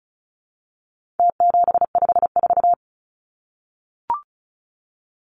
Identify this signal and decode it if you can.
Morse code, heard as a keyed tone, about 34 wpm, 700 Hz. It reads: T754
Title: Numbers - Contest Style - 4-Digit 35wpm